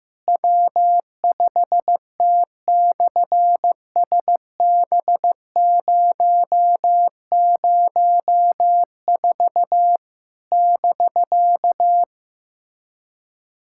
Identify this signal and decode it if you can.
Morse code, heard as a keyed tone, about 15 wpm, 700 Hz. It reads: W5T/SB004 <BK>
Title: SOTA QSO Practice 15wpm